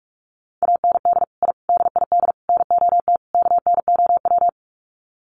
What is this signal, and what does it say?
Morse code, heard as a keyed tone, about 44 wpm, 700 Hz. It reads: AND I DID NOT KNOW
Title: Sets of 5 Words Encore 45wpm